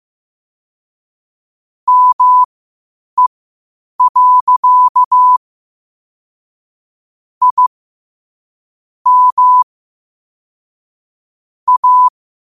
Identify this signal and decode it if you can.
Morse code, heard as a keyed tone, about 15 wpm, spaced as for 8 wpm, 1000 Hz. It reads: ME. I M A
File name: qotdfiles / 8wpm